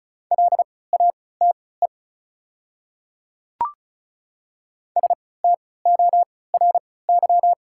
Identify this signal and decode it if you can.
Morse code, heard as a keyed tone, about 34 wpm, spaced as for 20 wpm, 700 Hz. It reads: LATE STORY
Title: 3x Character Spacing - Top 300 Words - 35wpm